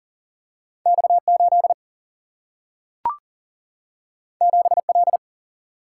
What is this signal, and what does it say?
Morse code, heard as a keyed tone, about 40 wpm, 700 Hz. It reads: X8 7L
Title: Single-Letter Single-Number 40wpm